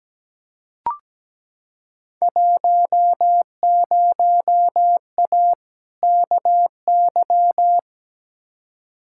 Morse code, heard as a keyed tone, about 17 wpm, 700 Hz. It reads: 10A KY
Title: Field Day Exchanges Encore - 17wpm